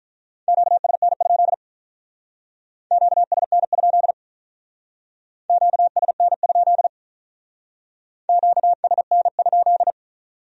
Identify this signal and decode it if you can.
Morse code, heard as a keyed tone, about 44 wpm, 700 Hz. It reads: QSN? QSN? QSN? QSN?